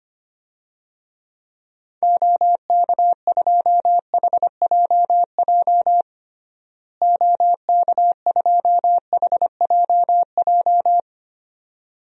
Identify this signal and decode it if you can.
Morse code, heard as a keyed tone, about 25 wpm, 700 Hz. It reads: OK2HJJ OK2HJJ
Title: DXCC-100 Call Signs - 6 Characters - Repeated 2x - 25wpm